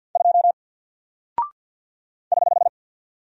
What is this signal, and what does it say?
Morse code, heard as a keyed tone, about 50 wpm, 700 Hz. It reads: 2 <HH>
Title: Single Character (Letter:Number:Punctuation:Prosign) Alt 50wpm